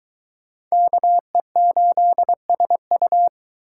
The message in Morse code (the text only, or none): KE8SU